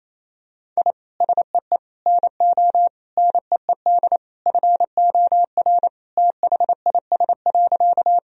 Morse code, heard as a keyed tone, about 28 wpm, 700 Hz. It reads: I SEE NO NEED FOR THIS.
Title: Sentences from Top 1000 Words - Review Part 1 - 28wpm